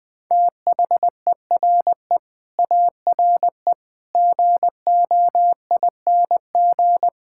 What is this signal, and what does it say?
Morse code, heard as a keyed tone, about 20 wpm, 700 Hz. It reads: THERE ARE GOING
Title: Sentences with 8-Word Phrases - 20wpm